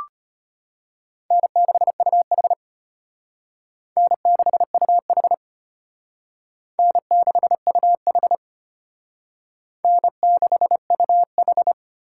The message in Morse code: N6UH N6UH N6UH N6UH

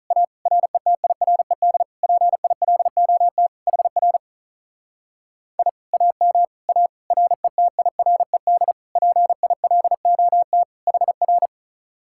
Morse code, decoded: A RETIRED PILOT HR I AM A RETIRED PILOT HR

